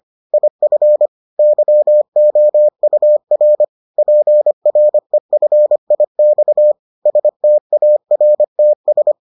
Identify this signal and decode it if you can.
Morse code, heard as a keyed tone, about 25 wpm, 600 Hz. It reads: IF YOUR PREFIX STARTS